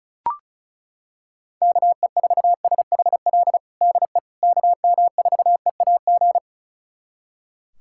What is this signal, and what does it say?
Morse code, heard as a keyed tone, about 36 wpm, 690 Hz. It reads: KE4SHL DE KM4EAG
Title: USA Callsign DE Callsign - 6 Characters - 35wpm